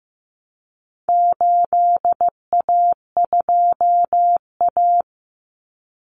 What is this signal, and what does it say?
Morse code, heard as a keyed tone, about 15 wpm, 700 Hz. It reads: 8A2A
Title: Common Cut Numbers - 4-Digit 15wpm